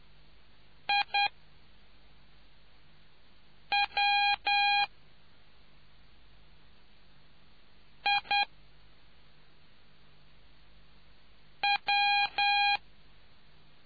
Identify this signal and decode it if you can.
Morse code, heard as a keyed tone, about 10 wpm, 810 Hz. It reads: I W I W